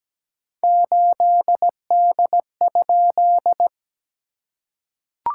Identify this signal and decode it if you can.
Morse code, heard as a keyed tone, about 17 wpm, 690 Hz. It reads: 8D?
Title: SOTA QSO Practice 17wpm